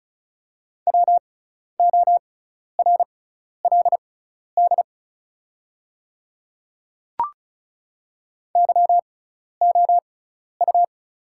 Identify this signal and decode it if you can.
Morse code, heard as a keyed tone, about 36 wpm, spaced as for 12 wpm, 690 Hz. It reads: WORLD YOU
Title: 6x Character Spacing - Top 200 Words - 35wpm